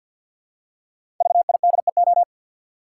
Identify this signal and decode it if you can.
Morse code, heard as a keyed tone, about 53 wpm, 700 Hz. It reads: VIDEO
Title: Five Letter Words - 50wpm